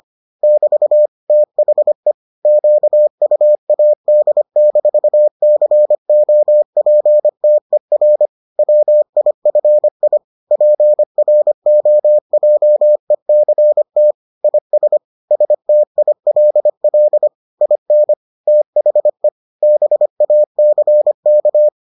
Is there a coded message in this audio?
Yes: Morse code from 0.4 to 21.7 s, heard as a keyed tone, about 25 wpm, 600 Hz. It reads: <BT> THE QUAD-COPTER WIFI PROJECT IS STILL IN THE BACK